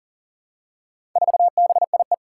Morse code, heard as a keyed tone, about 40 wpm, 700 Hz. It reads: 4BIE
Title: Four Challenging Letter-Number Combinations 40wpm